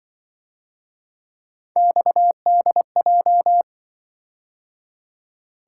Morse code, heard as a keyed tone, about 23 wpm, 700 Hz. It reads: XDJ
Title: Speed-Racing - Three Letter-Number Combinations - 20wpm